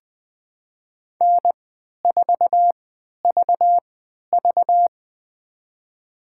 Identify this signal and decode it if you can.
Morse code, heard as a keyed tone, about 20 wpm, spaced as for 11 wpm, 700 Hz. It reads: N4VV
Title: USA Call Signs - 4 Characters - 3x Character Spacing - 20wpm